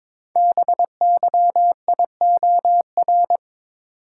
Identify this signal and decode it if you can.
Morse code, heard as a keyed tone, about 22 wpm, 700 Hz. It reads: BYIOR
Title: Code Groups - 5 Letters - 22wpm